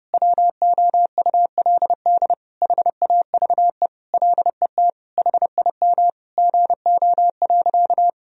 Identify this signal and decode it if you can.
Morse code, heard as a keyed tone, about 30 wpm, 700 Hz. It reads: WOULD HAVE LET HIM GO.
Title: Sentences from Top 1000 Words - Review Part 1 - 30wpm